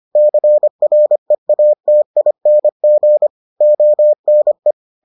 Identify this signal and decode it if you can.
Morse code, heard as a keyed tone, about 25 wpm, 600 Hz. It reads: CREATING ONE